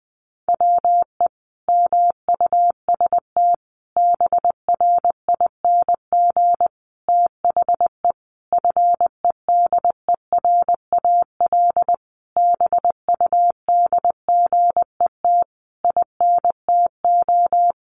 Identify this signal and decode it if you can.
Morse code, heard as a keyed tone, about 19 wpm, 700 Hz. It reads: WE MUST BRING THE FEDERAL BUDGET INTO
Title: Sentences from Top 1200 Words 20wpm